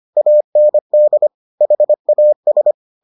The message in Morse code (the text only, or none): AND HAS